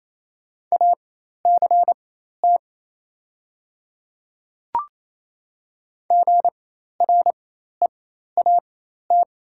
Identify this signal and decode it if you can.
Morse code, heard as a keyed tone, about 28 wpm, spaced as for 13 wpm, 700 Hz. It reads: ACT GREAT
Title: 4x Character Spacing - Top 500 Words - 28wpm